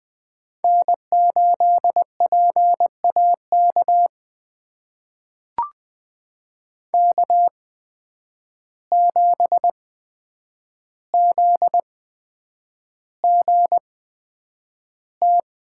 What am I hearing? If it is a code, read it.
Morse code, heard as a keyed tone, about 20 wpm, spaced as for 9 wpm, 700 Hz. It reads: N8PAK K7ZGT
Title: USA Call Signs - 5 Characters - 8x Character Spacing - 20wpm